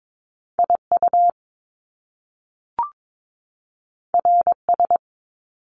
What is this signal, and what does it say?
Morse code, heard as a keyed tone, about 22 wpm, 700 Hz. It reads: IU RS
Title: Top Two Letter Combinations in Words 22wpm